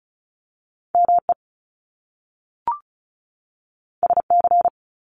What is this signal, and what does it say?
Morse code, heard as a keyed tone, about 36 wpm, 700 Hz. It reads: ME SC
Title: Top Two Letter Combinations in Words 35wpm